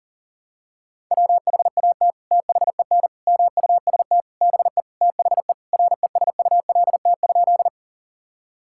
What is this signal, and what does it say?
Morse code, heard as a keyed tone, about 40 wpm, 700 Hz. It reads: WHAT THEN MUST BE THE RESULT?
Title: Sentences from Top 1000 Words - Review Part 5 - 40wpm